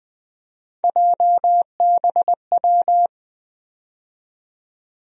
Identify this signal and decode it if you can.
Morse code, heard as a keyed tone, about 20 wpm, 700 Hz. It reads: JBW
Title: USA Call Sign Postfixes - 3 Letters 20wpm